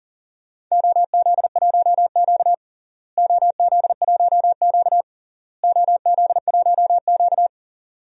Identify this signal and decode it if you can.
Morse code, heard as a keyed tone, about 40 wpm, 700 Hz. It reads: OZ1Q OZ1Q OZ1Q
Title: DXCC-100 Call Signs - 4 Characters - Repeated 3x - 40wpm